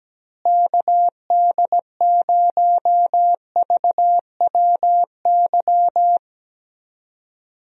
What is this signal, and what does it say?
Morse code, heard as a keyed tone, about 17 wpm, 700 Hz. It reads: KD0VWY
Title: USA Call Signs - 6 Characters - Repeated 2x - 17wpm